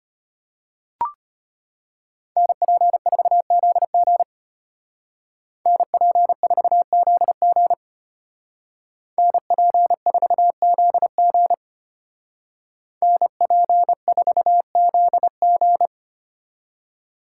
Morse code, heard as a keyed tone, about 33 wpm, 700 Hz. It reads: NP4ZG NP4ZG NP4ZG NP4ZG